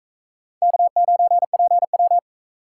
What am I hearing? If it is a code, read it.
Morse code, heard as a keyed tone, about 44 wpm, 700 Hz. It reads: K9PW